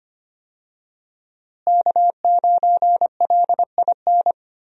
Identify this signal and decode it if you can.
Morse code, heard as a keyed tone, about 25 wpm, 700 Hz. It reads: K9LIN
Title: USA Call Signs - 5 Characters - Repeated 3x - 25wpm